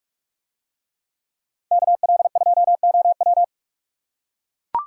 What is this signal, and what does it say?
Morse code, heard as a keyed tone, about 44 wpm, 700 Hz. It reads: KL2OW